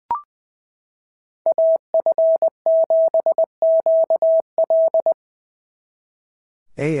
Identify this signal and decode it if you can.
Morse code, heard as a keyed tone, about 20 wpm, 690 Hz. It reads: AF7QL
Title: SOTA QSO Practice 20wpm